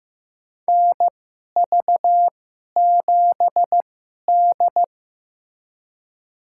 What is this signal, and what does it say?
Morse code, heard as a keyed tone, about 15 wpm, spaced as for 11 wpm, 690 Hz. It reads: NV7D
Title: USA Call Signs - 4 Characters - 2x Character Spacing - 15wpm